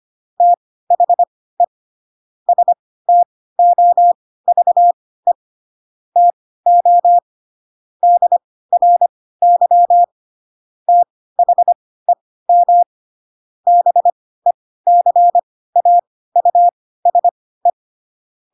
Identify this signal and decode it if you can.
Morse code, heard as a keyed tone, about 25 wpm, spaced as for 16 wpm, 700 Hz. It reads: THE STOVE TO DRY THEM BECAUSE
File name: Pinocchio / Pinocchio ch7 10 25 ik1hge